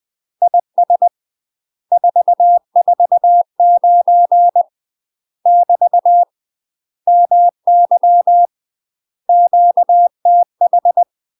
Morse code, heard as a keyed tone, about 20 wpm, 700 Hz. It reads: IS 449 <BT> MY QTH
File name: UK MASTER QSOs TRAINING / 20WPM / UK MASTER QSOs 0062